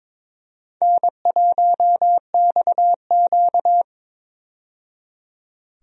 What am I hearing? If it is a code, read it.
Morse code, heard as a keyed tone, about 22 wpm, 700 Hz. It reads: N1XQ
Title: USA Call Signs - 4 Characters - 22wpm